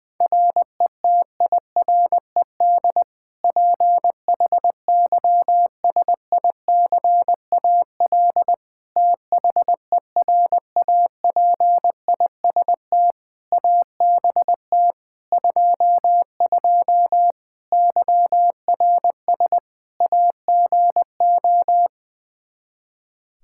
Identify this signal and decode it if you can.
Morse code, heard as a keyed tone, about 20 wpm, 700 Hz. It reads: RETIRED PHYSICAL THERAPIST ABT 22 YRS AGO
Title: Practice QSO - Exchange 3 - More Info - 20wpm